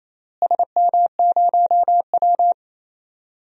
Morse code, heard as a keyed tone, about 28 wpm, 700 Hz. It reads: SM0W
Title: DXCC-100 Call Signs - 4 Characters - Repeated 3x - 28wpm